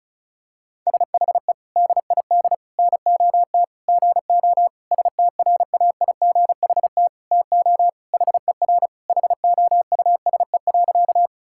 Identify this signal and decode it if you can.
Morse code, heard as a keyed tone, about 34 wpm, 700 Hz. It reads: SHE DID NOT GO STRAIGHT TO HER HOUSE.
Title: Sentences from Top 1000 Words - Review Part 6 - 35wpm